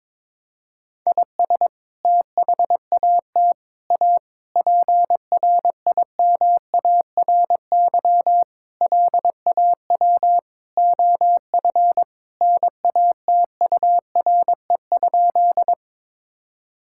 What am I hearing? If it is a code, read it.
Morse code, heard as a keyed tone, about 22 wpm, 700 Hz. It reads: IS THAT A PRIMARY LAW OF NATURE?
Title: Sentences from Top 1000 Words - Review Part 5 - 22wpm